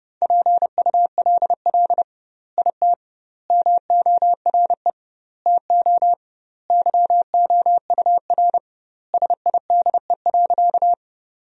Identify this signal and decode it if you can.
Morse code, heard as a keyed tone, about 30 wpm, 700 Hz. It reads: PULL IT MORE TO YOUR SIDE.
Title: 2x Word-Spacing - Sentences from Top 400 Words - 30wpm